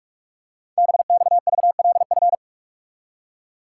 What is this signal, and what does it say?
Morse code, heard as a keyed tone, about 44 wpm, 700 Hz. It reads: BXVLF
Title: Code Groups - 5 Letters - 45wpm